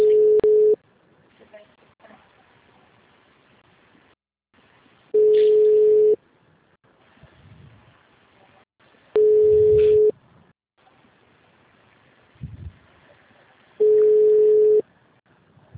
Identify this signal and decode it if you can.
Ringback tone (Europe, 425 Hz).
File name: out- -2025- - - .14844